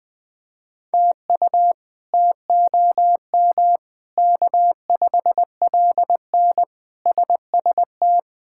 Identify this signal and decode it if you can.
Morse code, heard as a keyed tone, about 20 wpm, 700 Hz. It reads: TU TOM K5LN SST
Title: K1USN SST QSO Practice - 20wpm